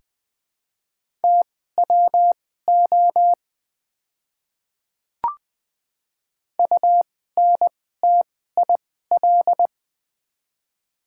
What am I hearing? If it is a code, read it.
Morse code, heard as a keyed tone, about 20 wpm, spaced as for 14 wpm, 700 Hz. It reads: TWO UNTIL
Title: 2x Character Spacing - Top 500 Words - 20wpm